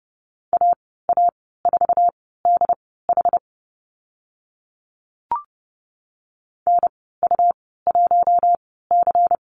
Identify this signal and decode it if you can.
Morse code, heard as a keyed tone, about 30 wpm, spaced as for 17 wpm, 700 Hz. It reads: AA4DH NU1C
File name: Lesson 19g - Letter w - Call Signs 30wpm